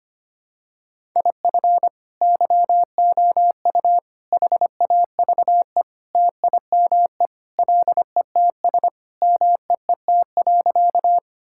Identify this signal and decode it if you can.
Morse code, heard as a keyed tone, about 25 wpm, 700 Hz. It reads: IF YOU HAVE TIME LETS MEET.
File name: practice / 25wpm / Sentences with 3-Word Phrases - 25wpm